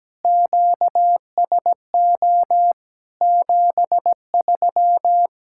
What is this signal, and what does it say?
Morse code, heard as a keyed tone, about 17 wpm, 700 Hz. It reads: QSO 73
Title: K1USN SST QSO Practice 17wpm